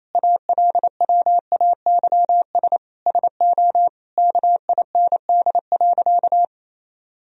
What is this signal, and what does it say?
Morse code, heard as a keyed tone, about 28 wpm, 700 Hz. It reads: ALWAYS SO KIND.
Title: Sentences from Top 1000 Words - Review Part 3 - 28wpm